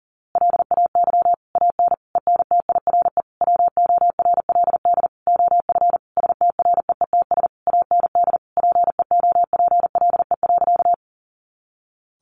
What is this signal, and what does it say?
Morse code, heard as a keyed tone, about 40 wpm, 700 Hz. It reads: LAY AN ENTIRE WORLD OF STREETS AND PEOPLE.
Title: A Sound Of Thunder 40wpm